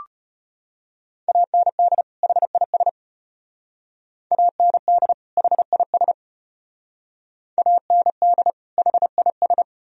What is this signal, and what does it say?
Morse code, heard as a keyed tone, about 34 wpm, 700 Hz. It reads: AND HIS AND HIS AND HIS